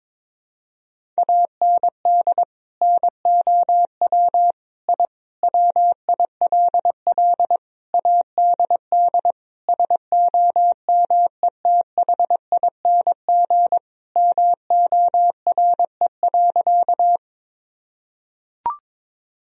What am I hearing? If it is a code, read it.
Morse code, heard as a keyed tone, about 22 wpm, 700 Hz. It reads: AND NOW I WILL ADD SOMETHING MORE.
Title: Sentences from Top 1000 Words - Review Part 1 - 22wpm